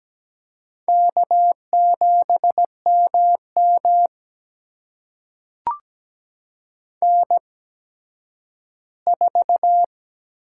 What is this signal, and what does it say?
Morse code, heard as a keyed tone, about 17 wpm, spaced as for 10 wpm, 700 Hz. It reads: K7MM N4
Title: USA Call Signs - 4 Characters - 8x Character Spacing - 17wpm